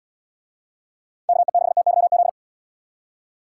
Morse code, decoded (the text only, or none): KE6TYZ